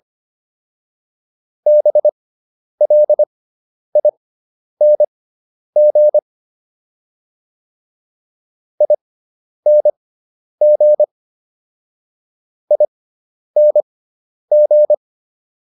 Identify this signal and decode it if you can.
Morse code, heard as a keyed tone, about 25 wpm, spaced as for 10 wpm, 600 Hz. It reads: BLING ING ING
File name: ING5 10